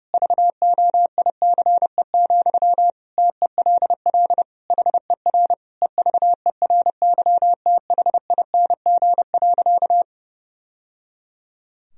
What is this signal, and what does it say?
Morse code, heard as a keyed tone, about 30 wpm, 700 Hz. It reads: VOICE, TELL HER EVERYTHING.
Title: Sentences from Top 1000 Words - Review Part 1 - 30wpm